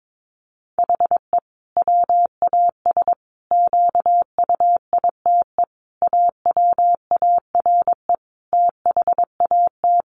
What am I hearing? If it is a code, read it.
Morse code, heard as a keyed tone, about 22 wpm, 700 Hz. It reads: HE WAS QUITE AWARE THAT
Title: Sentences from Top 1400 Words 22wpm